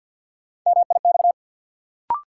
Morse code, heard as a keyed tone, about 53 wpm, 700 Hz. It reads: MIX
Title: Sentences from Top 1000 Words - Review Part 4 - New Word Drill - 50wpm